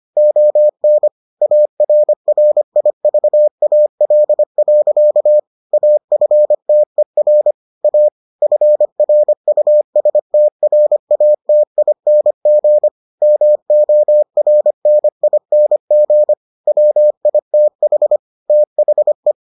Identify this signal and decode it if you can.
Morse code, heard as a keyed tone, about 25 wpm, 600 Hz. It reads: ON ARRIVAL. AFTER A FRUSTRATING MORNING WITH THE